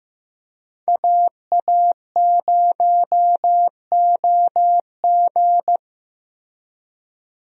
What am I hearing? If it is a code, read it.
Morse code, heard as a keyed tone, about 15 wpm, 700 Hz. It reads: AA0OG
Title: USA Call Signs - 5 Characters - Repeated 2x - 15wpm